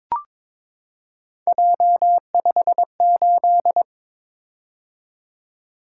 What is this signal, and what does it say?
Morse code, heard as a keyed tone, about 22 wpm, 700 Hz. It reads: J58
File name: practice / 22wpm / Three Letter-Number Combinations - 22wpm